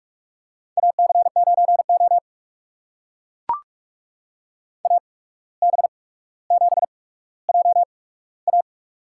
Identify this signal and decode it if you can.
Morse code, heard as a keyed tone, about 44 wpm, spaced as for 18 wpm, 700 Hz. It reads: AC9O AB7JA
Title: USA Call Signs - 8x Character Spacing - 45wpm